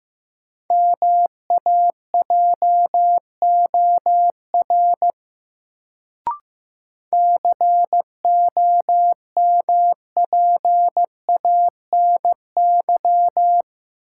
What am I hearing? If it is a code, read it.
Morse code, heard as a keyed tone, about 15 wpm, 700 Hz. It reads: MAJOR COMPANY
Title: Sentences from Top 600 Words - New Word Drill - 15wpm